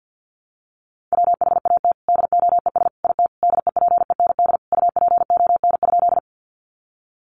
Morse code, heard as a keyed tone, about 50 wpm, 700 Hz. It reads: WHAT DOES IT DEPEND UPON?
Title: Sentences from Top 1300 Words 50wpm